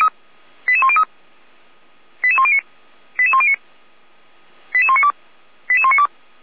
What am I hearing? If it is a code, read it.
Five-tone selective call (ZVEI), heard as quick runs of five tones, tones of about 70 ms. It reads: 88103 88109 88109 88192 88192
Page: /455/SDR/ZVEI/88102/